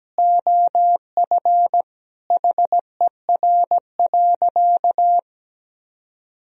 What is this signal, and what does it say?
Morse code, heard as a keyed tone, about 17 wpm, 700 Hz. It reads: OF HER.
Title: Sentences from Top 1000 Words - Review Part 1 - 17wpm